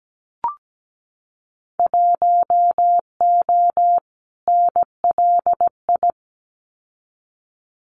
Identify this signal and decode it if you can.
Morse code, heard as a keyed tone, about 17 wpm, 700 Hz. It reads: 1O NLI
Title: Winter Field Day Exchanges 17wpm